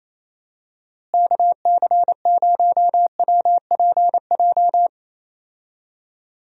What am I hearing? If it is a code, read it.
Morse code, heard as a keyed tone, about 28 wpm, 700 Hz. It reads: KC0WPJ
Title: USA Call Signs - 6 Characters - Repeated 2x - 28wpm